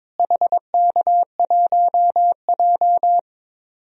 Morse code, heard as a keyed tone, about 22 wpm, 700 Hz. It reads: HK1J
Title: DXCC-100 Call Signs - 4 Characters - Repeated 2x - 22wpm